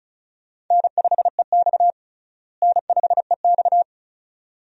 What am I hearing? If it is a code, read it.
Morse code, heard as a keyed tone, about 36 wpm, 700 Hz. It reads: N5EX N5EX
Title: USA Call Signs - 4 Characters - Repeated 2x - 35wpm